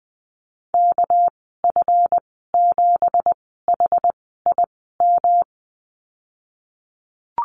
Morse code, heard as a keyed tone, about 20 wpm, spaced as for 14 wpm, 700 Hz. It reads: KF7HIM
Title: 200 Lesson 32g - Letter k - Call Signs 20wpm